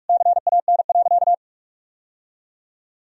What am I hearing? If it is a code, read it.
Morse code, heard as a keyed tone, about 46 wpm, 700 Hz. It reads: CAN.
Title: Sentences from Top 1000 Words - Review Part 3 - 45wpm